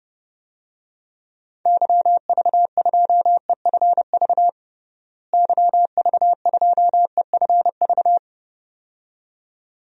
Morse code, heard as a keyed tone, about 32 wpm, 700 Hz. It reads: YV2EFV YV2EFV
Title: DXCC-100 Call Signs - 6 Characters - Repeated 2x - 30wpm